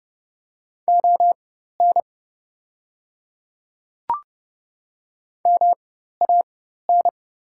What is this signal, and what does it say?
Morse code, heard as a keyed tone, about 30 wpm, spaced as for 14 wpm, 700 Hz. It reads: ON MAN
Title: 4x Character Spacing - Top 200 Words - 30wpm